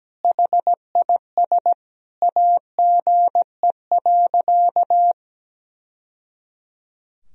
Morse code, heard as a keyed tone, about 17 wpm, 700 Hz. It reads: HIS AGE.
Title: Sentences from Top 1000 Words - Review Part 3 - 17wpm